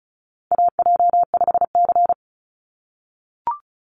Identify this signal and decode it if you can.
Morse code, heard as a keyed tone, about 34 wpm, 700 Hz. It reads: AJ5C
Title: NA SOTA Chaser Callsigns 35wpm